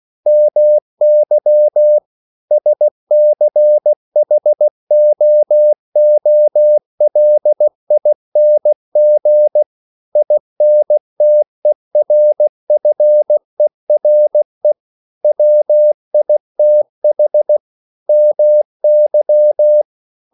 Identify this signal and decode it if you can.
Morse code, heard as a keyed tone, about 16 wpm, 600 Hz. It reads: MY SCHOOLING INTERFERE WITH MY
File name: CW / Practice / EnglishQuotations0416